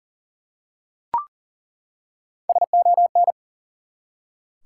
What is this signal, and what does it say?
Morse code, heard as a keyed tone, about 40 wpm, 700 Hz. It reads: SON
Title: Three Letter Words - 40wpm